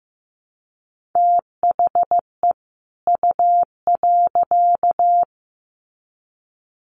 Morse code, heard as a keyed tone, about 15 wpm, 700 Hz. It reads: THE U.
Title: Sets of 2 Words 15wpm